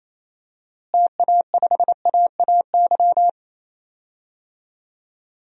Morse code, heard as a keyed tone, about 28 wpm, 700 Hz. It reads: TA5AAY
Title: DXCC-100 Call Signs - 6 Characters - Repeated 2x - 28wpm